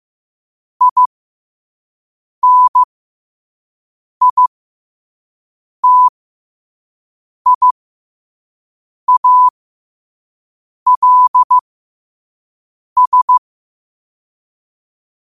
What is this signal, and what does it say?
Morse code, heard as a keyed tone, about 15 wpm, spaced as for 5 wpm, 1000 Hz. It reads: INITIALS
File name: qotdfiles / 5wpm